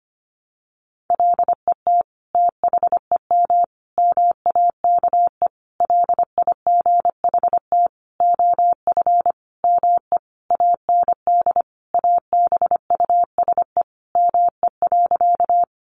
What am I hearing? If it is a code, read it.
Morse code, heard as a keyed tone, about 25 wpm, 700 Hz. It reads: LET THEM MAKE LIGHT OF ME AND ABUSE ME.
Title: Sentences from Top 1300 Words 25wpm